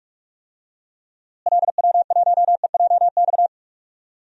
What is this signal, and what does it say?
Morse code, heard as a keyed tone, about 41 wpm, 700 Hz. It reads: LW1EJX
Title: DXCC-100 Call Signs - 6 Characters - Repeated 2x - 45wpm